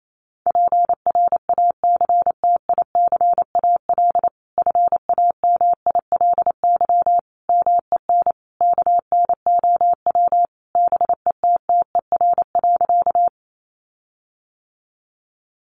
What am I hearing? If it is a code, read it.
Morse code, heard as a keyed tone, about 28 wpm, 700 Hz. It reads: PRACTICAL FAMILY MEN KNOW BETTER.
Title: Sentences from Top 1400 Words 28wpm